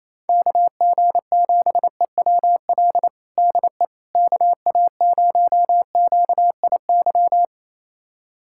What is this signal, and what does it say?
Morse code, heard as a keyed tone, about 28 wpm, 700 Hz. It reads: KG7EWL DE KA0QIY
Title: USA Callsign DE Callsign - 6 Characters - 28wpm